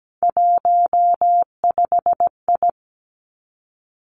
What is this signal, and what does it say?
Morse code, heard as a keyed tone, about 17 wpm, 700 Hz. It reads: 15I
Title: Three Challenging Letter-Number Combinations 17wpm